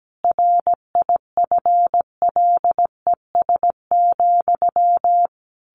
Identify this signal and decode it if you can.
Morse code, heard as a keyed tone, about 17 wpm, 700 Hz. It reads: RIFLES,
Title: A Sound Of Thunder 17wpm